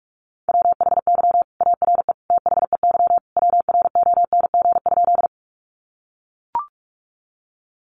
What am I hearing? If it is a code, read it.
Morse code, heard as a keyed tone, about 44 wpm, 700 Hz. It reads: WHY ARE THEY WRONG?
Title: Sentences from Top 900 Words 45wpm